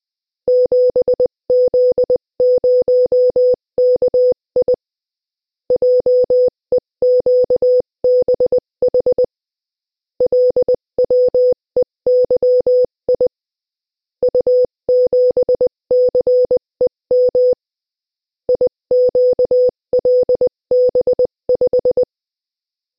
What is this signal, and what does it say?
Morse code, heard as a keyed tone, about 20 wpm, 500 Hz. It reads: 7Z0KI JEQBH LWEYI U7CEM IQLB5
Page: Приём на слух сигналов кода Морзе - Тестовое задание 3